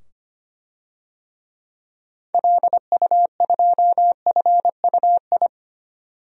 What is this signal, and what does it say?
Morse code, heard as a keyed tone, about 25 wpm, 700 Hz. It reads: LU2FUI
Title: DXCC-100 Call Signs - 6 Characters - 25wpm